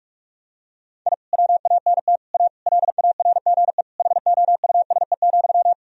Morse code, heard as a keyed tone, about 41 wpm, 700 Hz. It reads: I WANT A LARGE HOUSE,
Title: Sentences from Top 1000 Words - Review Part 1 - 45wpm